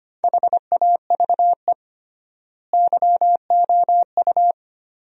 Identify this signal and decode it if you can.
Morse code, heard as a keyed tone, about 25 wpm, 700 Hz. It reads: HAVE YOU
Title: 3x Word-Spacing - Sentences from Top 400 Words - 25wpm